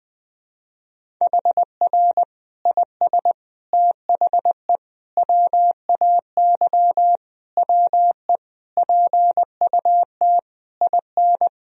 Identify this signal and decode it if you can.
Morse code, heard as a keyed tone, about 21 wpm, 700 Hz. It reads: HR IS THE WAY WE PUT IN
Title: Sentences from Top 200 Words and Abbreviations 20wpm